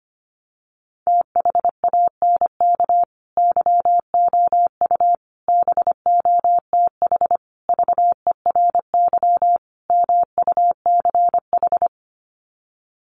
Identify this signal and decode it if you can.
Morse code, heard as a keyed tone, about 25 wpm, 700 Hz. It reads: THANK YOU BOTH VERY MUCH
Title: Sets of 5 Words 25wpm